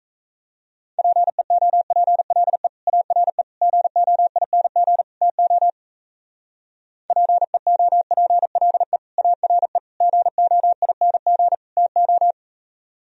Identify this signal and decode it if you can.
Morse code, heard as a keyed tone, about 35 wpm, 700 Hz. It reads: PEOPLE ARE GOING TO PEOPLE ARE GOING TO
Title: Speed-Racing - Sets of 4 Words 28wpm